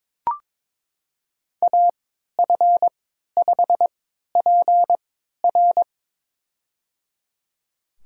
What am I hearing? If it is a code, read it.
Morse code, heard as a keyed tone, about 22 wpm, spaced as for 12 wpm, 700 Hz. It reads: AF5PR
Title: USA Call Signs - 5 Characters - 3x Character Spacing - 22wpm